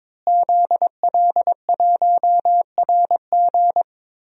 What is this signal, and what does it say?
Morse code, heard as a keyed tone, about 21 wpm, 700 Hz. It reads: ZL1RG
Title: DXCC-100 Call Signs - 5 Characters - Repeated 2x - 22wpm